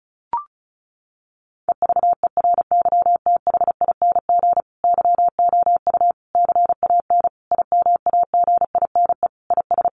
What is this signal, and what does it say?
Morse code, heard as a keyed tone, about 34 wpm, 700 Hz. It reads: EVERYTHING YOU CAN IMAGINE IS
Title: Sentences from Top 1400 Words 35wpm